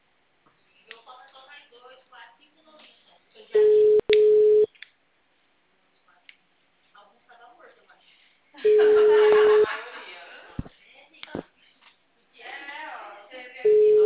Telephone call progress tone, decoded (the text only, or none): ringback tone